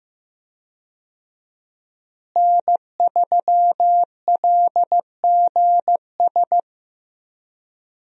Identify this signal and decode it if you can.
Morse code, heard as a keyed tone, about 15 wpm, 700 Hz. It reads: N3LGS
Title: USA Call Signs - 5 Characters - Repeated 3x - 15wpm